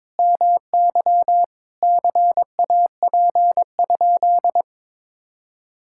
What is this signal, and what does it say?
Morse code, heard as a keyed tone, about 22 wpm, 700 Hz. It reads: MY CAP?
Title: Sentences from Top 1600 Words 22wpm